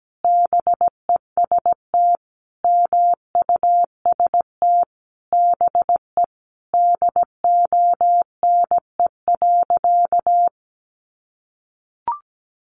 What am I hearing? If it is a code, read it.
Morse code, heard as a keyed tone, about 17 wpm, 700 Hz. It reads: BEST MUST BE DONE.
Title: Sentences from Top 1100 Words 17wpm